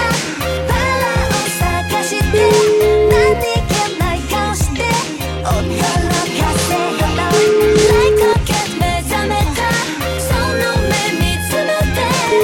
Ringback tone (Europe, 425 Hz), with noise.